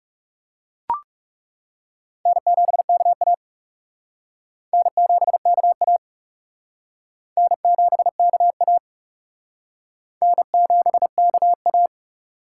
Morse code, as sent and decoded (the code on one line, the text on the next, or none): -. --... -.- .- / -. --... -.- .- / -. --... -.- .- / -. --... -.- .-
N7KA N7KA N7KA N7KA